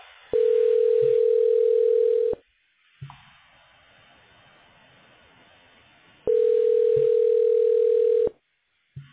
Ringback tone (North America), 440 + 480 Hz.